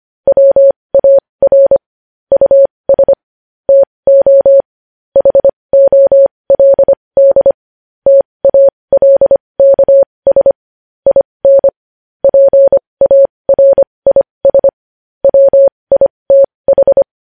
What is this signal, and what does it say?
Morse code, heard as a keyed tone, about 25 wpm, spaced as for 20 wpm, 550 Hz. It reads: WAR US TO HOLD TALKS IN PARIS WITH
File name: podcast / News headlines-1x-25x20